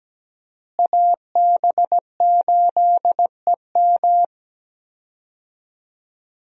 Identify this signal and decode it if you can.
Morse code, heard as a keyed tone, about 17 wpm, 700 Hz. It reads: AB8EM
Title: USA Call Signs - 5 Characters - Repeated 2x - 17wpm